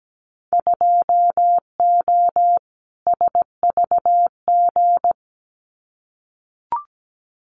Morse code, heard as a keyed tone, about 17 wpm, 700 Hz. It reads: 2O SVG
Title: Winter Field Day Exchanges 17wpm